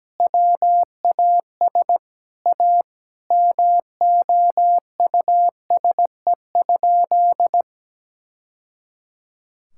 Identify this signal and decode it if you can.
Morse code, heard as a keyed tone, about 17 wpm, 700 Hz. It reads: WAS A MOUSE?
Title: Sentences from Top 1000 Words - Review Part 5 - 17wpm